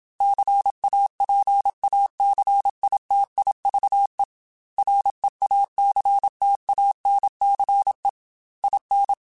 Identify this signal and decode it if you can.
Morse code, heard as a keyed tone, about 27 wpm, 800 Hz. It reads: CAPACITIVE REACTANCE IN
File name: Technical Subjects / Technical Subjects-mp3s / XC @25